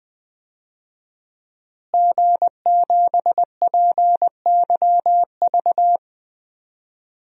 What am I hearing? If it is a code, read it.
Morse code, heard as a keyed tone, about 20 wpm, 700 Hz. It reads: G7PYV
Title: DXCC-100 Call Signs - 5 Characters - Repeated 2x - 20wpm